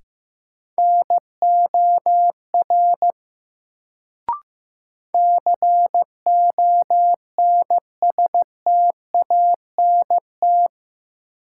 Morse code, heard as a keyed tone, about 15 wpm, 700 Hz. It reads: NOR CONSTANT
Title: Sentences from Top 1500 Words - New Word Drill - 15wpm